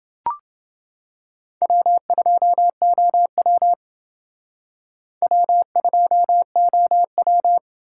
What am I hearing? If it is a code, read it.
Morse code, heard as a keyed tone, about 28 wpm, 700 Hz. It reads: W2OW W2OW